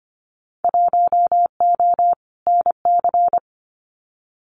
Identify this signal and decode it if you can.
Morse code, heard as a keyed tone, about 25 wpm, 700 Hz. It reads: 1O NC
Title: Winter Field Day Exchanges 25wpm